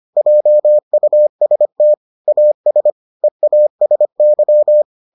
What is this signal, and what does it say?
Morse code, heard as a keyed tone, about 25 wpm, 600 Hz. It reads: JUST AS EASY